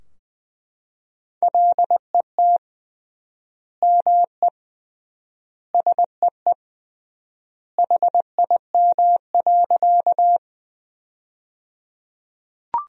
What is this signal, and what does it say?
Morse code, heard as a keyed tone, about 21 wpm, 700 Hz. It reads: LET ME SEE HIM.
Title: 3x Word-Spacing - Sentences from Top 300 Words - 20wpm